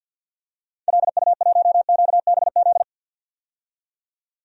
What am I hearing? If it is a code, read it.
Morse code, heard as a keyed tone, about 51 wpm, 700 Hz. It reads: LU1QBZ
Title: DXCC-100 Call Signs - 6 Characters - 50wpm